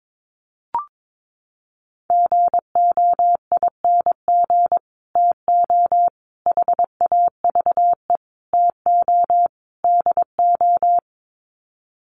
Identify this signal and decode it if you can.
Morse code, heard as a keyed tone, about 22 wpm, 700 Hz. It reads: GOING TO HAVE TO DO
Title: Sets of 5 Words Encore 22wpm